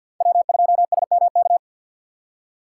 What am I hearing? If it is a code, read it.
Morse code, heard as a keyed tone, about 47 wpm, 700 Hz. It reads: W2SMK
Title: USA Call Signs - 4x Character Spacing - 50wpm